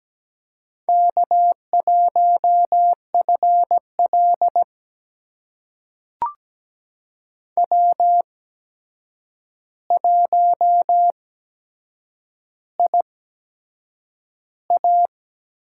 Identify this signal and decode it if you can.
Morse code, heard as a keyed tone, about 17 wpm, spaced as for 7 wpm, 700 Hz. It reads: K1FL W1IA
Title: USA Call Signs - 4 Characters - 8x Character Spacing - 17wpm